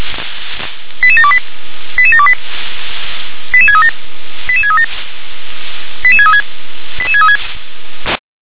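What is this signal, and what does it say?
Five-tone selective call (ZVEI), heard as quick runs of five tones, tones of about 70 ms. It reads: 88728 88728 88637 88637 88636 88636
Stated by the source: Five-tone ZVEI selective call received by SDR.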